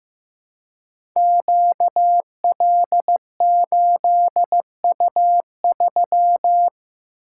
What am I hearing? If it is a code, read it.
Morse code, heard as a keyed tone, about 15 wpm, 700 Hz. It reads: QL8U3
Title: Code Groups - 5 Letter:Numbers - 15wpm